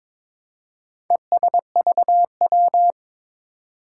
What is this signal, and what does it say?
Morse code, heard as a keyed tone, about 21 wpm, 700 Hz. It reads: ESVW
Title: Four Challenging Letter-Number Combinations 22wpm